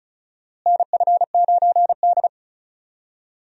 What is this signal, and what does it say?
Morse code, heard as a keyed tone, about 34 wpm, 700 Hz. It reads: NF9D
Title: USA Call Signs - 4 Characters - Repeated 2x - 35wpm